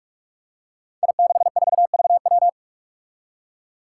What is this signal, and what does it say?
Morse code, heard as a keyed tone, about 46 wpm, 700 Hz. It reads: I64VW
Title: Five Challenging Letter-Number Combinations 45wpm